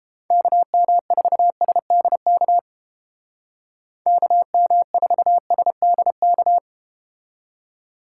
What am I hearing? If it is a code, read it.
Morse code, heard as a keyed tone, about 31 wpm, 700 Hz. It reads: KM4SDK KM4SDK